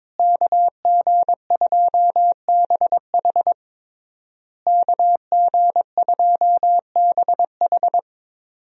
Morse code, heard as a keyed tone, about 22 wpm, 700 Hz. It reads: KG2BH KG2BH
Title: USA Call Signs - 5 Characters - Repeated 3x - 22wpm